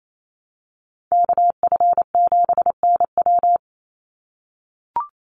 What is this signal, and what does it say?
Morse code, heard as a keyed tone, about 28 wpm, 700 Hz. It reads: KF7NW
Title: USA Call Signs 28wpm